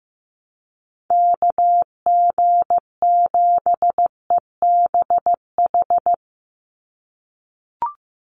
Morse code, heard as a keyed tone, about 15 wpm, 700 Hz. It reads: KG7EBH
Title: USA Call Signs 15wpm